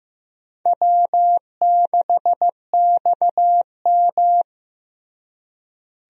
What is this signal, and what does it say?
Morse code, heard as a keyed tone, about 15 wpm, 700 Hz. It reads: W6XM
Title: USA Call Signs - 4 Characters - Repeated 2x - 15wpm